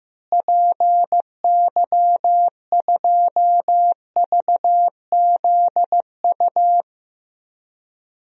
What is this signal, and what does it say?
Morse code, heard as a keyed tone, about 15 wpm, 700 Hz. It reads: PY2VZU
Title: DXCC-100 Call Signs - 6 Characters - Repeated 3x - 15wpm